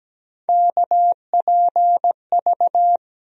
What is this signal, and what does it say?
Morse code, heard as a keyed tone, about 17 wpm, 700 Hz. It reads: KPV
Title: USA Call Sign Postfixes - 3 Letters 17wpm